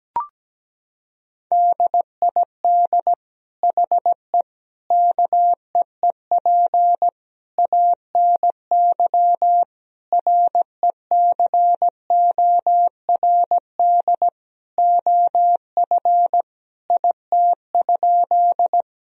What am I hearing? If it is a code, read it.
Morse code, heard as a keyed tone, about 17 wpm, 700 Hz. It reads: DID HE KEEP ANY RECORD OF IT?
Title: Sentences from Top 1000 Words - Review Part 3 - 17wpm